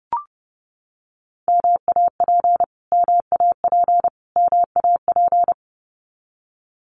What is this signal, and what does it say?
Morse code, heard as a keyed tone, about 30 wpm, 700 Hz. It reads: MAP MAP MAP
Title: Top 500 Words Repeated 30wpm